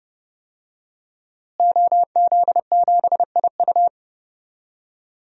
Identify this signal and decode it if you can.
Morse code, heard as a keyed tone, about 30 wpm, 700 Hz. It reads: OZ7IU
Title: DXCC-100 Call Signs - 5 Characters - Repeated 2x - 30wpm